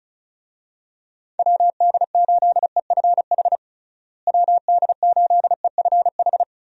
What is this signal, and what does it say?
Morse code, heard as a keyed tone, about 34 wpm, 700 Hz. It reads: WD8EFH WD8EFH
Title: USA Call Signs - 6 Characters - Repeated 3x - 35wpm